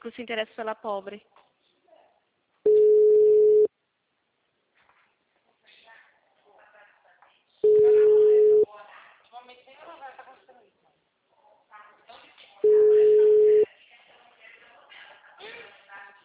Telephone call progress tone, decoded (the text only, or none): ringback tone